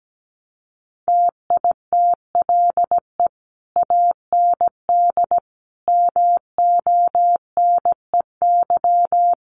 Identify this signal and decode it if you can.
Morse code, heard as a keyed tone, about 17 wpm, 700 Hz. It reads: TITLE AND MONEY
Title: Sentences from Top 1200 Words 17wpm